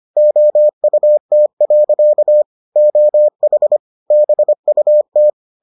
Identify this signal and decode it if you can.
Morse code, heard as a keyed tone, about 25 wpm, 600 Hz. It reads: OUT. OH BUT